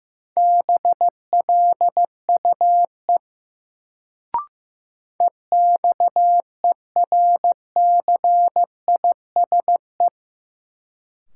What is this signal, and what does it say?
Morse code, heard as a keyed tone, about 15 wpm, 700 Hz. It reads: BLUE EXERCISE
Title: Sentences from Top 1200 Words - New Word Drill Alt - 15wpm